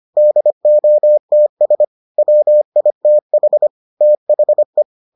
Morse code, heard as a keyed tone, about 25 wpm, 600 Hz. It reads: DOTS WITH THE